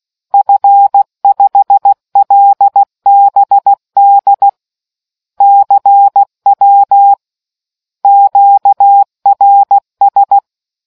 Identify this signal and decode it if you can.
Morse code, heard as a keyed tone, about 16 wpm, 800 Hz. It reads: F5LBD CW QRS